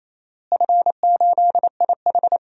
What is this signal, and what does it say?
Morse code, heard as a keyed tone, about 28 wpm, 700 Hz. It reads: F8IH
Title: DXCC-100 Call Signs - 4 Characters - Repeated 3x - 28wpm